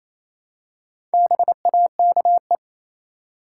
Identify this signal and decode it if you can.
Morse code, heard as a keyed tone, about 28 wpm, 700 Hz. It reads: BAKE
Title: Four Letter Words - 28wpm